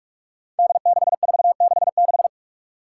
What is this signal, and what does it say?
Morse code, heard as a keyed tone, about 44 wpm, 700 Hz. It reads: D6466
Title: Five Challenging Letter-Number Combinations 45wpm